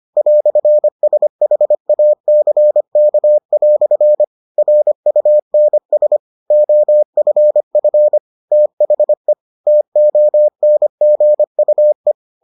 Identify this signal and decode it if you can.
Morse code, heard as a keyed tone, about 25 wpm, 600 Hz. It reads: "SHACK" RUNS OFF THE TONGUE